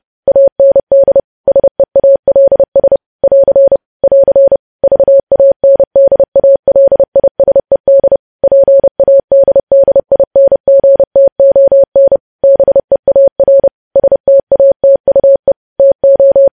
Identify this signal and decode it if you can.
Morse code, heard as a keyed tone, about 30 wpm, 550 Hz. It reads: AND SEALS <AR> <AR> VANDALISED PADDINGTON BEAR STATUE TO
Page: news headlines in morse code at 30 wpm